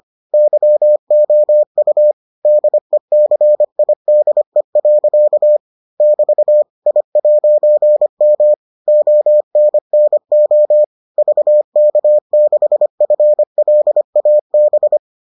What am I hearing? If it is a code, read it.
Morse code, heard as a keyed tone, about 25 wpm, 600 Hz. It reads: YOU DECIDE. <BT> I'M ONNO VK6FLAB